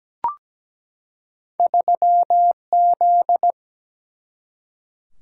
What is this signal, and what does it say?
Morse code, heard as a keyed tone, about 17 wpm, 700 Hz. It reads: 3Z
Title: Single-Letter Single-Number 17wpm